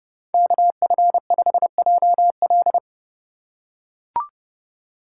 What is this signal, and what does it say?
Morse code, heard as a keyed tone, about 32 wpm, 700 Hz. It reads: KF5JL
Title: USA Call Signs - 5 Characters - 2x Character Spacing - 30wpm